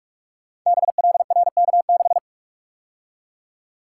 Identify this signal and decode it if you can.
Morse code, heard as a keyed tone, about 46 wpm, 700 Hz. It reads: BLRK6
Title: Code Groups - 5 Letter:Numbers - 45wpm